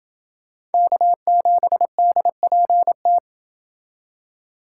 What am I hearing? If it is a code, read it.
Morse code, heard as a keyed tone, about 29 wpm, 700 Hz. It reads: K7DPT